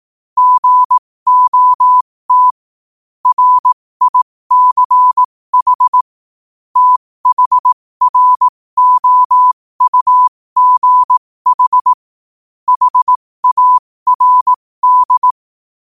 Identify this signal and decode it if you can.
Morse code, heard as a keyed tone, about 18 wpm, spaced as for 15 wpm, 1000 Hz. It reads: GOT RICH THROUGH HARD
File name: qotdfiles / 15wpm